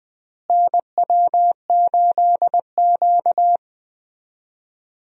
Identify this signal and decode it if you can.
Morse code, heard as a keyed tone, about 21 wpm, 700 Hz. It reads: NW8Q
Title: USA Call Signs - 4 Characters - Repeated 3x - 20wpm